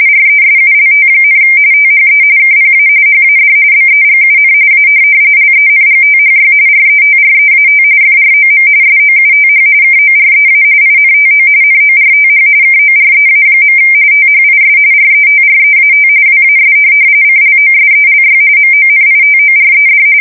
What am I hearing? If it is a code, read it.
RTTY, about 45.45 baud. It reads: RESULT, THE KPA HAD BEGUN A LENGTHY RETREAT FROM SOUTH KOREA INTO NORTH KOREA. THIS RETREAT WAS CLOSELY MONITORED
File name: fsk170 / USS-Missouri